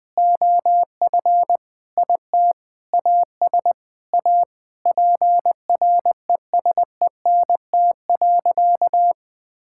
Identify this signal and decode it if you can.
Morse code, heard as a keyed tone, about 20 wpm, 700 Hz. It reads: OF IT AS A PRESENT.
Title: Sentences from Top 1000 Words - Review Part 5 - 20wpm